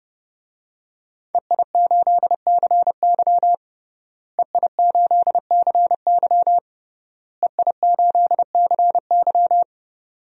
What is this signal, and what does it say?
Morse code, heard as a keyed tone, about 32 wpm, 700 Hz. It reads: EI8CY EI8CY EI8CY
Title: DXCC-100 Call Signs - 5 Characters - Repeated 3x - 30wpm